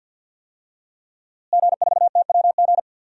Morse code, heard as a keyed tone, about 51 wpm, 700 Hz. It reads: G4TWG
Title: DXCC-100 Call Signs - 5 Characters - Repeated 2x - 50wpm